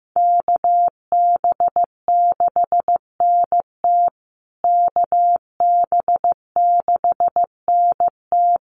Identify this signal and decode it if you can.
Morse code, heard as a keyed tone, about 15 wpm, 700 Hz. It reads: KB6NT KB6NT
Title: USA Call Signs Repeated 3x 15wpm